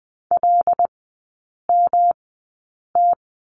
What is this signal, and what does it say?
Morse code, heard as a keyed tone, about 20 wpm, spaced as for 8 wpm, 700 Hz. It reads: LMT
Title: 121 Lesson 20e - Number 3 - Three Characters 20wpm